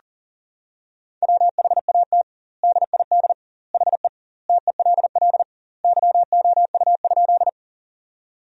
Morse code, heard as a keyed tone, about 40 wpm, 700 Hz. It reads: WHAT DID HE TELL YOU?
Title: 2x Word-Spacing - Sentences from Top 200 Words - 40wpm